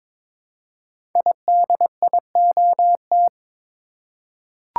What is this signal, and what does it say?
Morse code, heard as a keyed tone, about 22 wpm, 690 Hz. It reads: IDIOT
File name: 5 Letter Words - First 3 Letters Given - 22wpm